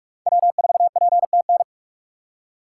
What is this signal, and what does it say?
Morse code, heard as a keyed tone, about 43 wpm, 700 Hz. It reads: W4PTN
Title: USA Call Signs - 5 Characters - 6x Character Spacing - 45wpm